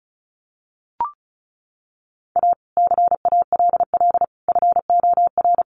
Morse code, heard as a keyed tone, about 34 wpm, 700 Hz. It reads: A CALL FOR
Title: Sentences from Top 300 Words 35wpm